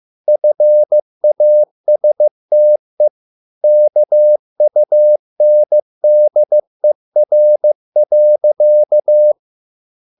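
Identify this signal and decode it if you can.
Morse code, heard as a keyed tone, about 15 wpm, 600 Hz. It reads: FASTE KUNDER.
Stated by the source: Morse code at a slow 15 wpm from CW med Gnister.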